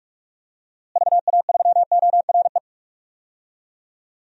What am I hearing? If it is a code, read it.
Morse code, heard as a keyed tone, about 47 wpm, 700 Hz. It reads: VA3ORE
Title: DXCC-100 Call Signs - 6 Characters - Repeated 2x - 45wpm